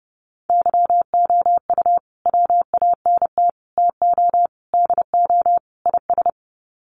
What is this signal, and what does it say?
Morse code, heard as a keyed tone, about 30 wpm, 700 Hz. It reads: YOU WANT TO DO IS
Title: Sets of 5 Words Encore2 30wpm